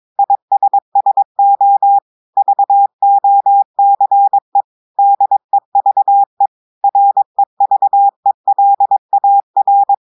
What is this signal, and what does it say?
Morse code, heard as a keyed tone, about 22 wpm, 800 Hz. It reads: ISSO VOCE DEVE REVELAR
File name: audios / portugais / portugais texte en clair 2 vitesse 22 mots minutes